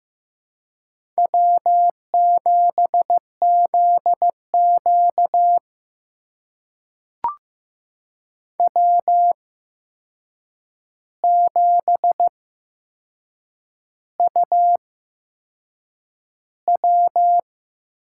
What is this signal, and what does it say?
Morse code, heard as a keyed tone, about 15 wpm, spaced as for 6 wpm, 700 Hz. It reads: W7ZQ W7UW
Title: USA Call Signs - 4 Characters - 8x Character Spacing - 15wpm